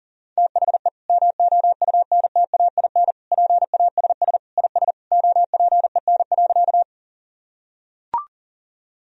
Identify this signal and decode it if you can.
Morse code, heard as a keyed tone, about 40 wpm, 700 Hz. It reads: THE MOUNTAIN PASS IS OPEN.
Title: Sentences from Top 1000 Words - Review Part 1 - 40wpm